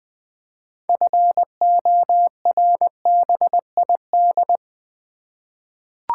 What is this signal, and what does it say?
Morse code, heard as a keyed tone, about 20 wpm, 700 Hz. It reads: FORBID
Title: 6 Letter Words - First 3 Letters Given - 20wpm